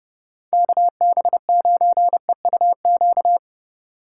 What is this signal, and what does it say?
Morse code, heard as a keyed tone, about 30 wpm, 700 Hz. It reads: KB9EUQ
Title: USA Call Signs - 6 Characters - 3x Character Spacing - 30wpm